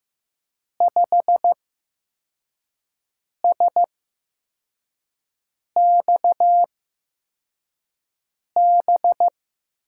Morse code, heard as a keyed tone, about 15 wpm, spaced as for 4 wpm, 700 Hz. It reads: 5SXB
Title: USA Call Signs - 6 Characters - 8x Character Spacing - 15wpm